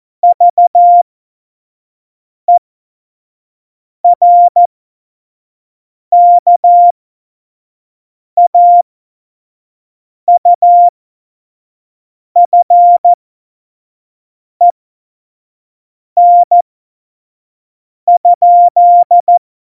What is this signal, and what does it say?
Morse code, heard as a keyed tone, about 14 wpm, spaced as for 5 wpm, 700 Hz. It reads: VERKAUFEN?
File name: Morsekurs / Witze / Witz 15. Januar 2019, 25 F70